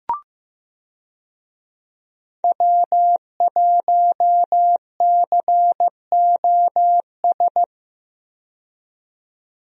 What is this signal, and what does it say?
Morse code, heard as a keyed tone, about 15 wpm, 700 Hz. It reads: W1COS
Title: USA Call Signs - 5 Characters - 15wpm